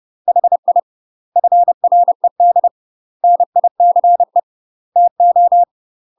Morse code, heard as a keyed tone, about 30 wpm, 700 Hz. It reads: HI FRED NICE TO
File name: UK MASTER QSOs TRAINING / 30WPM / UK MASTER QSOs 0026